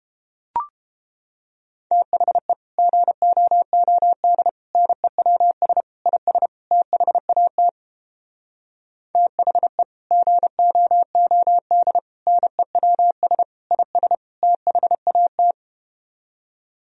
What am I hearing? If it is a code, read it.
Morse code, heard as a keyed tone, about 33 wpm, 700 Hz. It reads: THE GOOD NEWS IS THAT THE GOOD NEWS IS THAT